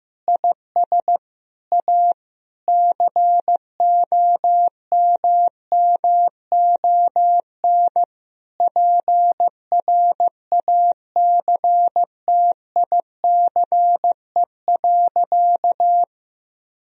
Morse code, heard as a keyed tone, about 15 wpm, 700 Hz. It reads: IS A COMMON PRACTICE.
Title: Sentences from Top 1000 Words - Review Part 4 - 15wpm